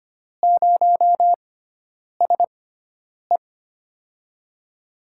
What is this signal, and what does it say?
Morse code, heard as a keyed tone, about 25 wpm, spaced as for 9 wpm, 700 Hz. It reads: 0SE
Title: USA Call Signs - 4 Characters - 6x Character Spacing - 25wpm